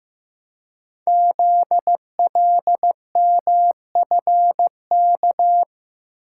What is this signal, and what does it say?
Morse code, heard as a keyed tone, about 15 wpm, 700 Hz. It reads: ZLMFK
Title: Code Groups - 5 Letters - 15wpm